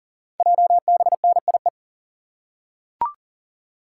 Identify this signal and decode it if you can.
Morse code, heard as a keyed tone, about 38 wpm, 690 Hz. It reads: JBNIE
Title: Code Groups - 5 Letters - 40wpm